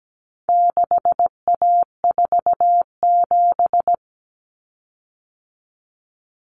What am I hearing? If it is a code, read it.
Morse code, heard as a keyed tone, about 17 wpm, 700 Hz. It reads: 6A47
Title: Common Cut Numbers - 4-Digit 17wpm